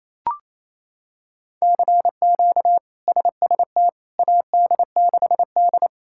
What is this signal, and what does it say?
Morse code, heard as a keyed tone, about 28 wpm, 700 Hz. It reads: CQ SST AD6D
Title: K1USN SST QSO Practice - 28wpm